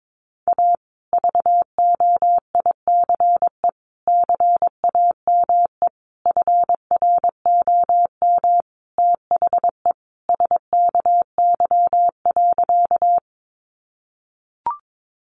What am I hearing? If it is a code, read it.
Morse code, heard as a keyed tone, about 22 wpm, 700 Hz. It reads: A VOICE CAME FROM THE SKY.
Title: Sentences from Top 1200 Words 22wpm